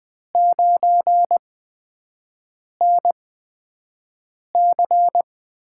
Morse code, heard as a keyed tone, about 20 wpm, spaced as for 5 wpm, 700 Hz. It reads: 9NC
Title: USA Call Signs - 6 Characters - 8x Character Spacing - 20wpm